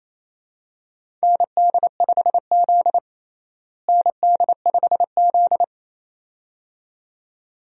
Morse code, heard as a keyed tone, about 28 wpm, 700 Hz. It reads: ND5Z ND5Z
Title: USA Call Signs - 4 Characters - Repeated 2x - 28wpm